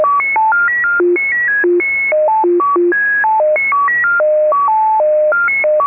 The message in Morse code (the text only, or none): none